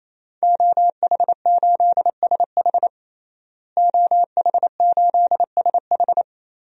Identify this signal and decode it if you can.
Morse code, heard as a keyed tone, about 27 wpm, 700 Hz. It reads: OH8SH OH8SH
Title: DXCC-100 Call Signs - 5 Characters - Repeated 3x - 28wpm